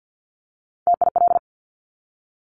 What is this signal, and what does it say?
Morse code, heard as a keyed tone, about 50 wpm, 700 Hz. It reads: TIL
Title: Top Three Letter Combinations in Words 50wpm